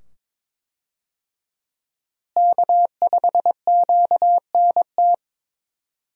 Morse code, heard as a keyed tone, about 23 wpm, 700 Hz. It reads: K5QNT
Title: USA Call Signs - 5 Characters - Repeated 2x - 22wpm